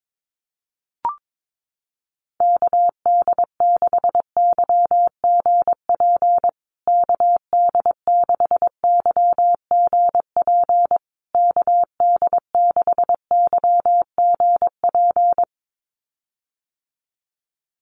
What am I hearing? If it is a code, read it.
Morse code, heard as a keyed tone, about 22 wpm, 700 Hz. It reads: KD6YGP KD6YGP KD6YGP
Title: USA Call Signs Repeated 3x 22wpm